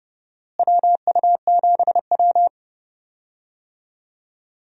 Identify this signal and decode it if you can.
Morse code, heard as a keyed tone, about 30 wpm, 700 Hz. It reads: WU7W
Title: USA Call Signs - 4 Characters - Repeated 3x - 30wpm